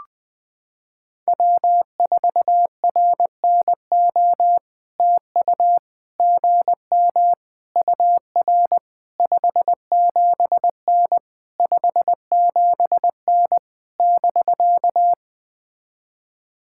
Morse code, heard as a keyed tone, about 20 wpm, 700 Hz. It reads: W4RNO TU GM UR 57N 57N <BK>
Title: POTA QSO Practice - 20wpm